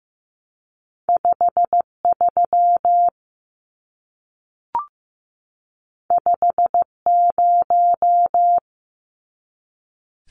Morse code, heard as a keyed tone, about 15 wpm, 700 Hz. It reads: 5 3 5 0
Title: Numbers - 2-Digit 15wpm